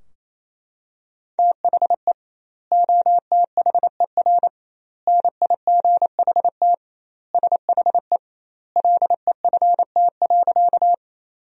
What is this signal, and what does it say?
Morse code, heard as a keyed tone, about 28 wpm, 700 Hz. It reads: THE OTHER NIGHT SHE LEFT.
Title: 2x Word-Spacing - Sentences from Top 300 Words - 28wpm